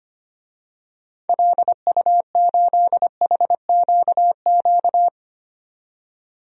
Morse code, heard as a keyed tone, about 25 wpm, 700 Hz. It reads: LU8HQQ
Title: DXCC-100 Call Signs - 6 Characters - 25wpm